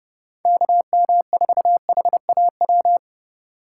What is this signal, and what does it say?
Morse code, heard as a keyed tone, about 29 wpm, 700 Hz. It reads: KM4HAW